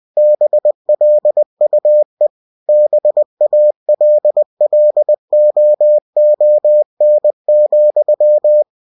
Morse code, heard as a keyed tone, about 20 wpm, 600 Hz. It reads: BLUE BALLOON,
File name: cw-books / winnie-the-pooh / winnie-20wpm-page-0006